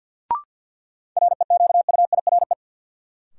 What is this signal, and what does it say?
Morse code, heard as a keyed tone, about 53 wpm, 700 Hz. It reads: REQUIRE
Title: Sentences from Top 900 Words - New Word Drill - 50wpm